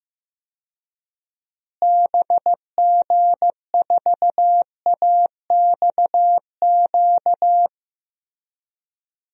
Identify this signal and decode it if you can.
Morse code, heard as a keyed tone, about 16 wpm, 700 Hz. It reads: BG4AXQ
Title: DXCC-100 Call Signs - 6 Characters - Repeated 2x - 15wpm